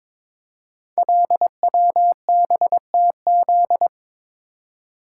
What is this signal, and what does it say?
Morse code, heard as a keyed tone, about 22 wpm, 690 Hz. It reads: LWBTZ
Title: Code Groups - 5 Letter:Numbers - 22wpm